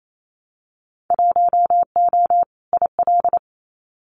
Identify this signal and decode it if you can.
Morse code, heard as a keyed tone, about 28 wpm, 700 Hz. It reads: 1O IL
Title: Winter Field Day Exchanges 28wpm